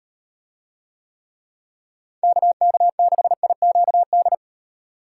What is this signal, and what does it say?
Morse code, heard as a keyed tone, about 38 wpm, 700 Hz. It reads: KK6IQD